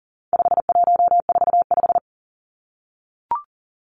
Morse code, heard as a keyed tone, about 40 wpm, 700 Hz. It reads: 5145
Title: Numbers - 4-Digit 40wpm